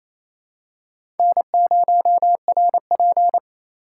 Morse code, heard as a keyed tone, about 28 wpm, 700 Hz. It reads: N0RP
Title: USA Call Signs - 4 Characters - 8x Character Spacing - 28wpm